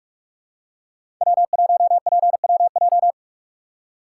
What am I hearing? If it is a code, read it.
Morse code, heard as a keyed tone, about 43 wpm, 700 Hz. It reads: W1PWJ
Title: USA Call Signs - 5 Characters - 3x Character Spacing - 45wpm